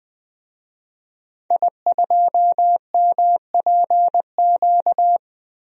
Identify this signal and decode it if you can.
Morse code, heard as a keyed tone, about 20 wpm, 700 Hz. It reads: I2MPQ
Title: DXCC-100 Call Signs - 5 Characters - 20wpm